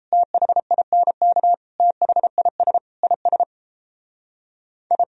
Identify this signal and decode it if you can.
Morse code, heard as a keyed tone, about 36 wpm, 700 Hz. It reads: THINK THIS IS I